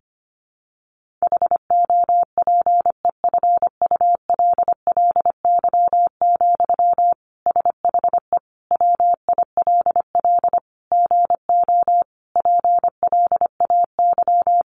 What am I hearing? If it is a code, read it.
Morse code, heard as a keyed tone, about 25 wpm, 700 Hz. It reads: HOPEFULLY, SHE WILL GO PLAY
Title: Sentences from Top 1500 Words 25wpm